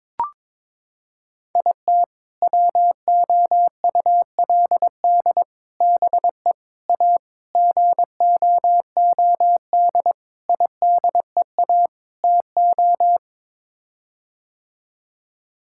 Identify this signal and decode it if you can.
Morse code, heard as a keyed tone, about 21 wpm, 700 Hz. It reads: IT WOULD BE A GOOD IDEA TO
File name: practice / 22wpm / Sets of 7 Words - 22wpm